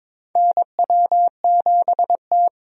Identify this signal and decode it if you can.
Morse code, heard as a keyed tone, about 22 wpm, 700 Hz. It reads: NW7T
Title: USA Call Signs - 4 Characters - 22wpm